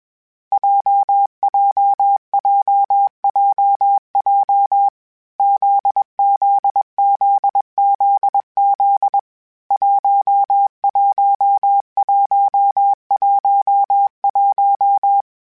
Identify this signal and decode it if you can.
Morse code, heard as a keyed tone, about 21 wpm, 800 Hz. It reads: JJJJJ ZZZZZ 11111
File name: Basic Letter Groups Collective7/JZ17@20